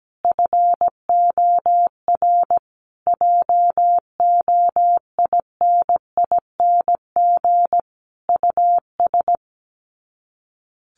Morse code, heard as a keyed tone, about 17 wpm, 700 Hz. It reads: FOR JOINING US
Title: Sets of 5 Words 17wpm